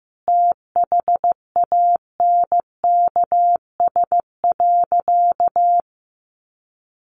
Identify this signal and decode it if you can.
Morse code, heard as a keyed tone, about 15 wpm, 700 Hz. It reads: THANKS.
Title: Sentences from Top 700 Words 15wpm